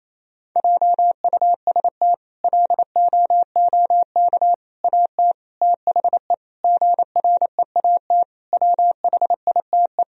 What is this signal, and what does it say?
Morse code, heard as a keyed tone, about 28 wpm, 700 Hz. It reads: JUST LOOK AT THE GREAT WHITE
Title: Sentences from Top 1000 Words - Review Part 1 - 28wpm